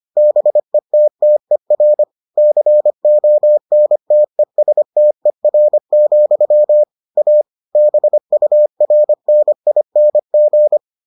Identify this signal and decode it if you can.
Morse code, heard as a keyed tone, about 25 wpm, 600 Hz. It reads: BETTER CONTESTER, A BURNING